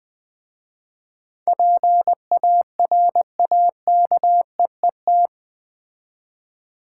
Morse code, heard as a keyed tone, about 20 wpm, 700 Hz. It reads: PARAKEET
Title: Difficult Words with Hints - 20wpm